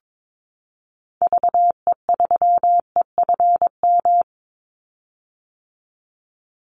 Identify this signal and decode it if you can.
Morse code, heard as a keyed tone, about 22 wpm, 700 Hz. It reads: VE3EFM
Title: Call Signs - 22wpm